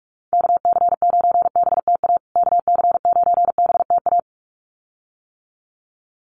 Morse code, heard as a keyed tone, about 44 wpm, 700 Hz. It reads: KC9BTA KC9BTA
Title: USA Call Signs Repeated 2x 45wpm